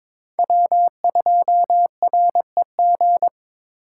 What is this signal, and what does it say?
Morse code, heard as a keyed tone, about 22 wpm, 700 Hz. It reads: W2REG
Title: USA Call Signs - 3x Character Spacing - 22wpm